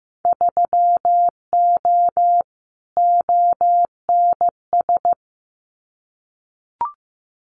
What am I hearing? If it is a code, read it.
Morse code, heard as a keyed tone, about 15 wpm, 700 Hz. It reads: 3O ONS
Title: Winter Field Day Exchanges Encore 15wpm